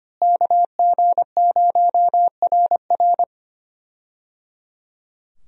Morse code, heard as a keyed tone, about 25 wpm, 700 Hz. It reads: KG0RR
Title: USA Call Signs - 5 Characters - Repeated 3x - 25wpm